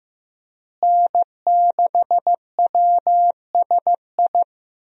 Morse code, heard as a keyed tone, about 15 wpm, 700 Hz. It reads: N6WSI
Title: USA Call Signs - 5 Characters - Repeated 2x - 15wpm